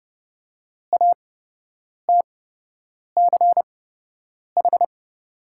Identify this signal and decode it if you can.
Morse code, heard as a keyed tone, about 30 wpm, spaced as for 8 wpm, 700 Hz. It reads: ATCH
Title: 8x Character Spacing - Top 300 Words - 30wpm